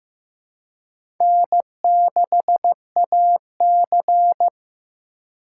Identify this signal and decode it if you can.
Morse code, heard as a keyed tone, about 15 wpm, 700 Hz. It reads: N6AC
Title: USA Call Signs - 2x Character Spacing - 15wpm